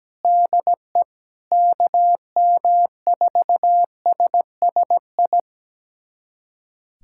Morse code, heard as a keyed tone, about 17 wpm, 700 Hz. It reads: DE KM4SSI
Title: USA Callsign DE Callsign 17wpm